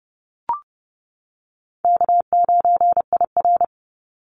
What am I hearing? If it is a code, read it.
Morse code, heard as a keyed tone, about 30 wpm, 700 Hz. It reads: K9IR
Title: NA SOTA Chaser Callsigns 30wpm